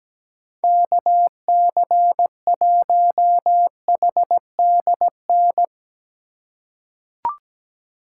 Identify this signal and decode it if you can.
Morse code, heard as a keyed tone, about 17 wpm, 700 Hz. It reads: KC1HDN
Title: USA Call Signs - 6 Characters - 4x Character Spacing - 17wpm